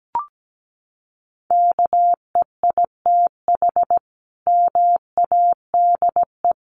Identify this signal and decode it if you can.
Morse code, heard as a keyed tone, about 17 wpm, 700 Hz. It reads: KEITH MADE
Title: A Sound Of Thunder 17wpm